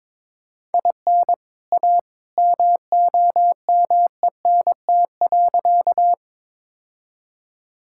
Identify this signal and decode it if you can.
Morse code, heard as a keyed tone, about 22 wpm, 700 Hz. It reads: IN A MOMENT.
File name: practice / 22wpm / Sentences with 2-Word Phrases - 22wpm